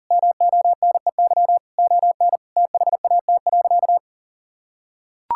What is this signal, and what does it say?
Morse code, heard as a keyed tone, about 40 wpm, 700 Hz. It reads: MONEY ON THAT.
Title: Sentences from Top 1600 Words 40wpm